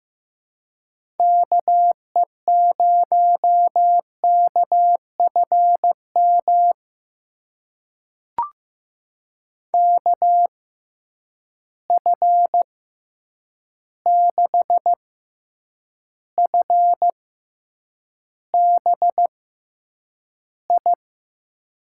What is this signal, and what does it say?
Morse code, heard as a keyed tone, about 15 wpm, spaced as for 8 wpm, 700 Hz. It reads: KE0KFM KF6FBI
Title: USA Call Signs - 6 Characters - 6x Character Spacing - 15wpm